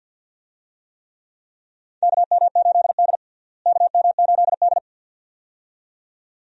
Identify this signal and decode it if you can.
Morse code, heard as a keyed tone, about 51 wpm, 700 Hz. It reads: KM8D KM8D
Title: USA Call Signs - 4 Characters - Repeated 2x - 50wpm